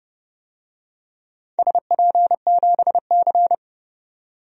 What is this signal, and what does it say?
Morse code, heard as a keyed tone, about 30 wpm, 700 Hz. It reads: SP7C
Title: DXCC-100 Call Signs - 4 Characters - 30wpm